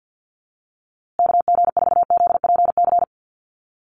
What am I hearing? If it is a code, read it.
Morse code, heard as a keyed tone, about 50 wpm, 700 Hz. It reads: KG4ZPC
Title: USA Call Signs Repeated 2x 50wpm